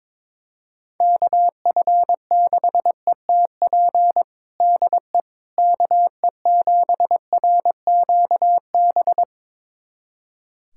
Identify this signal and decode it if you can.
Morse code, heard as a keyed tone, about 22 wpm, 700 Hz. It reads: KF6ETP DE KE7RQB
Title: USA Callsign DE Callsign - 6 Characters - 22wpm